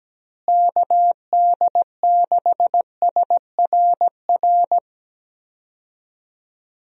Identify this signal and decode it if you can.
Morse code, heard as a keyed tone, about 17 wpm, 700 Hz. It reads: KD6SRR
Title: USA Call Signs - 6 Characters - Repeated 3x - 17wpm